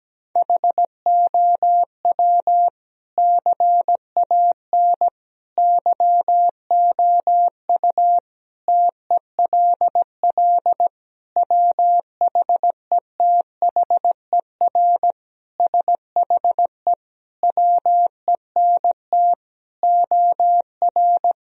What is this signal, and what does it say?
Morse code, heard as a keyed tone, about 17 wpm, 700 Hz. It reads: HOW CAN YOU TELL WHETHER SHE WENT OR
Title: Sentences from Top 1000 Words - Review Part 3 - 17wpm